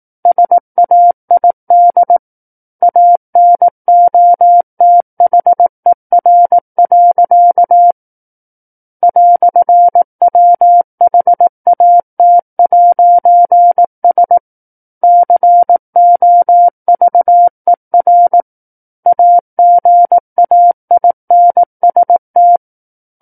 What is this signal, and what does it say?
Morse code, heard as a keyed tone, about 18 wpm, 700 Hz. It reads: SAID ANOTHER. "WHAT'S COVER AGAINST
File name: 18 WPM